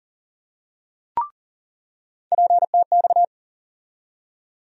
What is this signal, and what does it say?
Morse code, heard as a keyed tone, about 40 wpm, 700 Hz. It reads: PTX
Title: USA Call Sign Postfixes - 3 Letters 40wpm